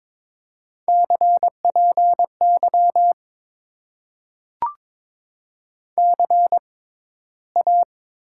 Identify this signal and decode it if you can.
Morse code, heard as a keyed tone, about 22 wpm, spaced as for 13 wpm, 700 Hz. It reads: CPY CA
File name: Top 90 QSO Elements - 6x Character Spacing - 22wpm